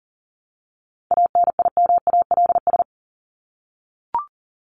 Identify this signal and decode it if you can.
Morse code, heard as a keyed tone, about 40 wpm, 700 Hz. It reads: ANIMALS
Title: ES:S Suffix - Top 100 Words with Base Words 40wpm